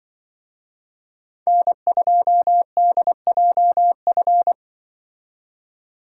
Morse code, heard as a keyed tone, about 25 wpm, 700 Hz. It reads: N2DJF
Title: Speed Racing - USA Call Signs 20wpm